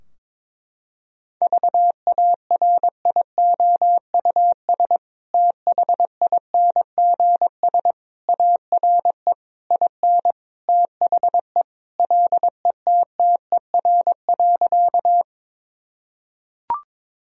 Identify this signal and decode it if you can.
Morse code, heard as a keyed tone, about 22 wpm, 700 Hz. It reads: VARIOUS THINGS ARE IN THE LETTER.
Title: Sentences from Top 1000 Words - Review Part 4 - 22wpm